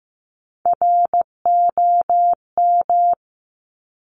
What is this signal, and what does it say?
Morse code, heard as a keyed tone, about 15 wpm, 700 Hz. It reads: ROM
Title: Top Three Letter Combinations in Words 15wpm